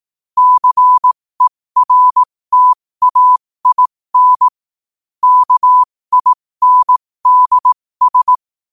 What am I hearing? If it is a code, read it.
Morse code, heard as a keyed tone, about 18 wpm, spaced as for 15 wpm, 1000 Hz. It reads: CERTAIN KINDS
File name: qotdfiles / 15wpm